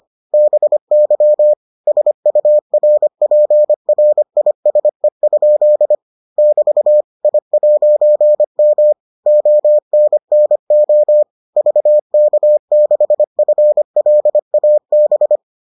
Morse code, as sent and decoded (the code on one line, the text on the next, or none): -... -.-- / ... ..- .-. .--. .-. .. ... ..--.. / -...- / .. .----. -- / --- -. -. --- / ...- -.- -.... ..-. .-.. .- -...
BY SURPRISE? <BT> I'M ONNO VK6FLAB